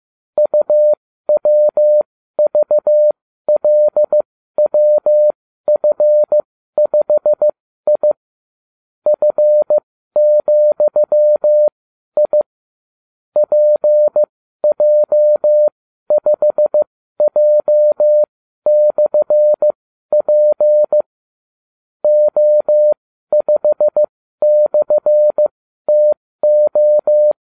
Morse code, heard as a keyed tone, about 15 wpm, spaced as for 12 wpm, 600 Hz. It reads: UWVLWF5I F,I PJ5J/P O5/TO